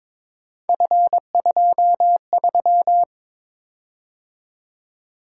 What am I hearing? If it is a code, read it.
Morse code, heard as a keyed tone, about 22 wpm, 700 Hz. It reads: F23
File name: practice / 22wpm / Three Letter-Number Combinations - 22wpm